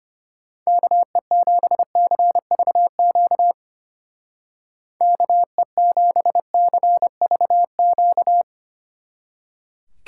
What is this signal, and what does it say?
Morse code, heard as a keyed tone, about 31 wpm, 700 Hz. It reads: KE7CVQ KE7CVQ
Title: Speed Racing - USA 2x3 Call Signs 25wpm